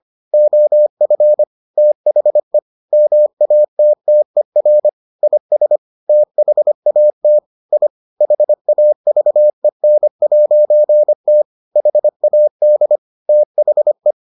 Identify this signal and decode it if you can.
Morse code, heard as a keyed tone, about 25 wpm, 600 Hz. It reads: OF THE MATTER IS THAT I HAVEN'T HAD THE